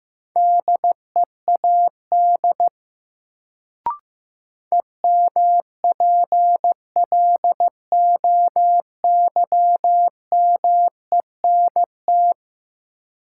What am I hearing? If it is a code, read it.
Morse code, heard as a keyed tone, about 15 wpm, 700 Hz. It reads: DEAD EMPLOYMENT
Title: Sentences from Top 1500 Words - New Word Drill - 15wpm